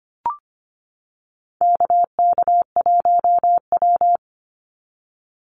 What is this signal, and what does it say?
Morse code, heard as a keyed tone, about 25 wpm, 700 Hz. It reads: KK1W
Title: NA SOTA Chaser Callsigns 25wpm